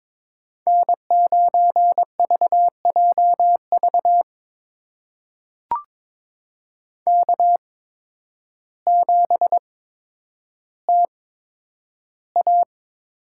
Morse code, heard as a keyed tone, about 21 wpm, spaced as for 10 wpm, 700 Hz. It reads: N9VJV K7TA
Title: USA Call Signs - 5 Characters - 8x Character Spacing - 22wpm